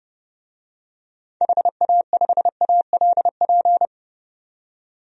Morse code, heard as a keyed tone, about 32 wpm, 700 Hz. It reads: HA5ALP
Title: DXCC-100 Call Signs - 6 Characters - 30wpm